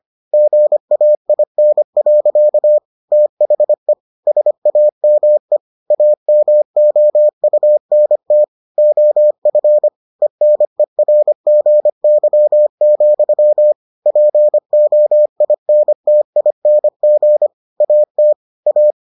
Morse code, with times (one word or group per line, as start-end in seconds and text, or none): 0.3-2.8 GAIN.
3.1-3.9 THE
4.3-5.6 SAME
5.9-8.4 AMOUNT
8.8-9.9 OF
10.2-13.7 ENERGY,
14.1-17.5 POINTING
17.8-18.3 AT
18.7-18.9 A